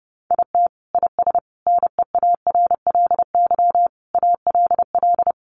Morse code, heard as a keyed tone, about 30 wpm, 700 Hz. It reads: IT IS NEARLY ALL
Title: Sentences from Top 1300 Words 30wpm